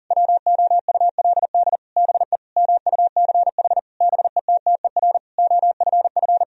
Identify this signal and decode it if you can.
Morse code, heard as a keyed tone, about 40 wpm, 700 Hz. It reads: WOULD BE MUCH BETTER OFF
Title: Sentences from Top 1000 Words - Review Part 3 - 40wpm